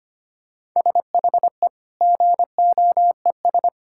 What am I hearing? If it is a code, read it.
Morse code, heard as a keyed tone, about 25 wpm, 700 Hz. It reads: SHE GOES
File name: practice / 25wpm / Sentences with 2-Word Phrases - 25wpm